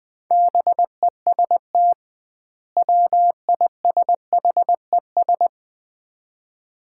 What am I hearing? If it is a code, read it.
Morse code, heard as a keyed tone, about 20 wpm, 700 Hz. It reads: BEST WISHES
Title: Common QSO Phrases - 2x Word-Spacing - 20wpm